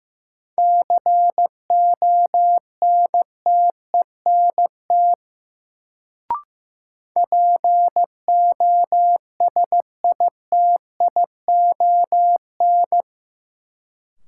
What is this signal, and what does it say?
Morse code, heard as a keyed tone, about 16 wpm, 700 Hz. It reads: CONTENT POSITION
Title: Sentences from Top 900 Words - New Word Drill - 15wpm